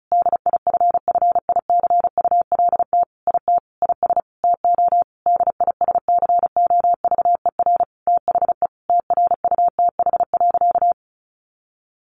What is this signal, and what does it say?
Morse code, heard as a keyed tone, about 34 wpm, 700 Hz. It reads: DIFFICULT IT IS TO DISCOVER THE TRUTH.
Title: Sentences from Top 1100 Words 35wpm